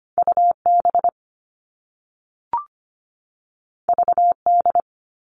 Morse code, heard as a keyed tone, about 25 wpm, 700 Hz. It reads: UB VD
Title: All Cut Numbers - 2-Digit 25wpm